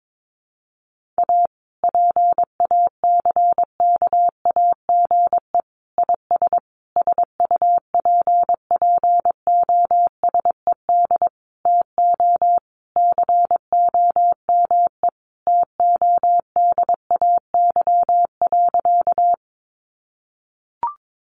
Morse code, heard as a keyed tone, about 22 wpm, 700 Hz. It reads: A PACKAGE IS SUPPOSED TO COME TODAY.
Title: Additional Easy Phrases 22wpm